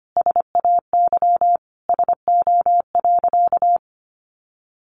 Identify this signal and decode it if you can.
Morse code, heard as a keyed tone, about 25 wpm, 700 Hz. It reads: SAY SO.
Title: Sentences from Top 1400 Words 25wpm